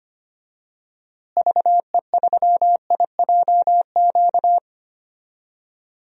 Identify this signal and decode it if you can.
Morse code, heard as a keyed tone, about 25 wpm, 700 Hz. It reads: VE3IJQ
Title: DXCC-100 Call Signs - 6 Characters - Repeated 2x - 25wpm